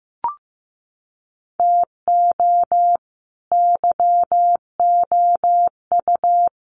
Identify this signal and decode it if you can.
Morse code, heard as a keyed tone, about 15 wpm, 700 Hz. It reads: TO YOU
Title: Sets of 2 Words 15wpm